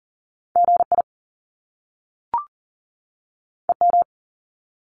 Morse code, heard as a keyed tone, about 40 wpm, 700 Hz. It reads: GI EM
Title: Top Two Letter Combinations in Words 40wpm